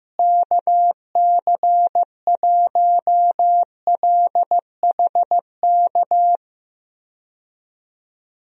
Morse code, heard as a keyed tone, about 15 wpm, 700 Hz. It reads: KC1LHK
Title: USA Call Signs - 6 Characters - Repeated 3x - 15wpm